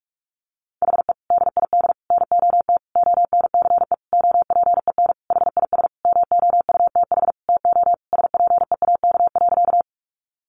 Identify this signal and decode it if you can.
Morse code, heard as a keyed tone, about 44 wpm, 700 Hz. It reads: HE DID NOT ONCE OPEN HIS MOUTH TO SPEAK.
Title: Sentences from Top 1500 Words 45wpm